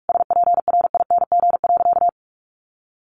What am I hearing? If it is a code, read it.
Morse code, heard as a keyed tone, about 44 wpm, 700 Hz. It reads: SPRING.
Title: Sentences from Top 1200 Words 45wpm